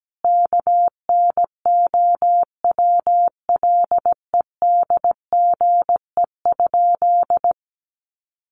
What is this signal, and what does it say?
Morse code, heard as a keyed tone, about 17 wpm, 700 Hz. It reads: KNOWLEDGE?
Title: Sentences from Top 600 Words 17wpm